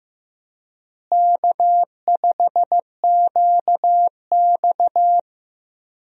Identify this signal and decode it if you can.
Morse code, heard as a keyed tone, about 15 wpm, 700 Hz. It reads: K5QX
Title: USA Call Signs - 4 Characters - Repeated 3x - 15wpm